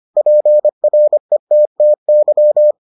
Morse code, heard as a keyed tone, about 25 wpm, 600 Hz. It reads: PRETTY